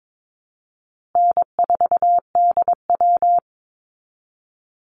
Morse code, heard as a keyed tone, about 22 wpm, 700 Hz. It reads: N4DW
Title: Call Signs Repeated 3x - 22wpm